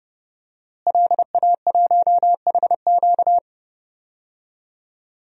Morse code, heard as a keyed tone, about 31 wpm, 700 Hz. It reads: LA1HQ
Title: DXCC-100 Call Signs - 5 Characters - Repeated 3x - 30wpm